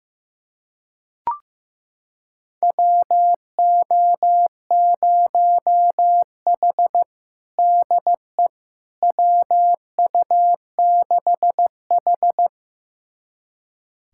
Morse code, heard as a keyed tone, about 15 wpm, 700 Hz. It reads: WO0H DE WU6H
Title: USA Callsign DE Callsign - 4 Characters - 15wpm